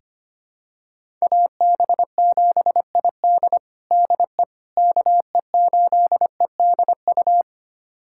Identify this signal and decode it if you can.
Morse code, heard as a keyed tone, about 25 wpm, 700 Hz. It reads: AB7ID DE KE8EDU
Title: USA Callsign DE Callsign 25wpm